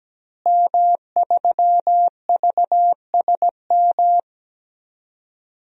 Morse code, heard as a keyed tone, about 17 wpm, 700 Hz. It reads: M3VSM
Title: DXCC-100 Call Signs - 5 Characters - Repeated 3x - 17wpm